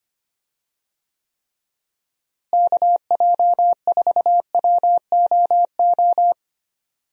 Morse code, heard as a keyed tone, about 25 wpm, 700 Hz. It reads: KJ4WOO
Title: USA Call Signs - 6 Characters - Repeated 3x - 25wpm